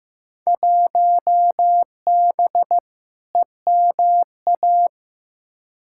Morse code, heard as a keyed tone, about 15 wpm, 700 Hz. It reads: 1B EMA
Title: Field Day Exchanges - 15wpm